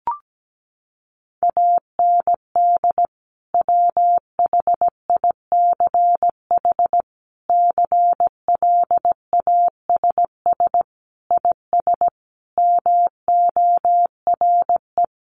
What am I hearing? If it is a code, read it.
Morse code, heard as a keyed tone, about 17 wpm, 700 Hz. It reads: AND WHICH CLASS IS MORE
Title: Sentences from Top 1400 Words 17wpm